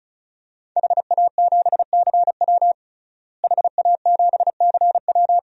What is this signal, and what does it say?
Morse code, heard as a keyed tone, about 36 wpm, 690 Hz. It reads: HA7CW HA7CW
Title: DXCC-100 Call Signs - 5 Characters - Repeated 3x - 35wpm